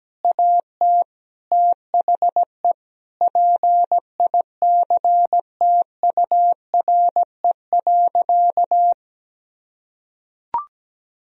Morse code, heard as a keyed tone, about 17 wpm, 700 Hz. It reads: AT THE PICTURE.
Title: Sentences from Top 1000 Words - Review Part 1 - 17wpm